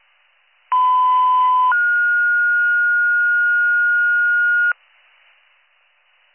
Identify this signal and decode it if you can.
Two-tone page, heard as a one-second tone, then a three-second tone, about 1010 then 1430 Hz. It is heard from 0.7 to 4.7 s.